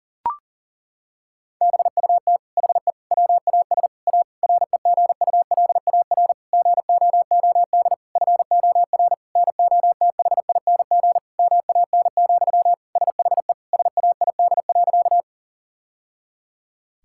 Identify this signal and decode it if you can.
Morse code, heard as a keyed tone, about 40 wpm, 700 Hz. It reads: BUT HE WAS A REGULAR GOOD FOR NOTHING MAN, SHE SAID.
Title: Sentences from Top 1000 Words - Review Part 6 - 40wpm